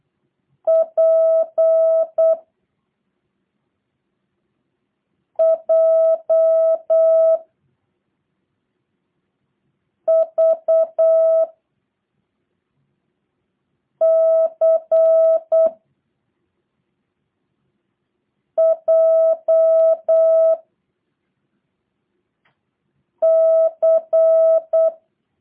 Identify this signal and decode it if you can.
Morse code, heard as a keyed tone, about 8 wpm, 650 Hz. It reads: P J V C J C
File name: jh8cbh / 2023cw / review1-5